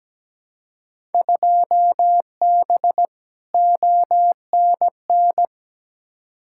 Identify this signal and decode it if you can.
Morse code, heard as a keyed tone, about 17 wpm, 700 Hz. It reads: 2B ONN
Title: Field Day Exchanges - 17wpm